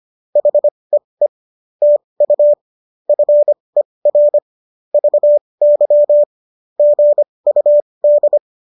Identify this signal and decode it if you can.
Morse code, heard as a keyed tone, about 25 wpm, spaced as for 20 wpm, 590 Hz. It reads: HEE TU FER VY GUD